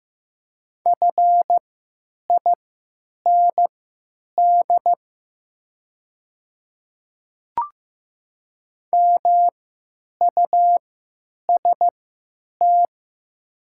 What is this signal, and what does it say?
Morse code, heard as a keyed tone, about 15 wpm, spaced as for 8 wpm, 700 Hz. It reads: FIND MUST
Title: 3x Character Spacing - Top 200 Words - 15wpm